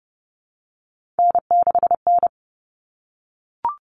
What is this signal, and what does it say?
Morse code, heard as a keyed tone, about 30 wpm, 700 Hz. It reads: N6N
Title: All Cut Numbers - 3-Digit 30wpm